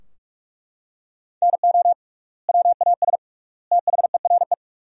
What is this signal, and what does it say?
Morse code, heard as a keyed tone, about 41 wpm, 700 Hz. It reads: NO WAS THERE
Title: 3x Word-Spacing - Sentences from Top 100 Words - 45wpm